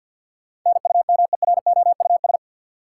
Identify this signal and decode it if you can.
Morse code, heard as a keyed tone, about 48 wpm, 700 Hz. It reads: NUMEROUS
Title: Sentences from Top 1400 Words - New Word Drill Alt - 50wpm